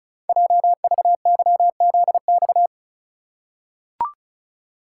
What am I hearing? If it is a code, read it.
Morse code, heard as a keyed tone, about 36 wpm, 700 Hz. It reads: JVYZX
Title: Code Groups - 5 Letters - 35wpm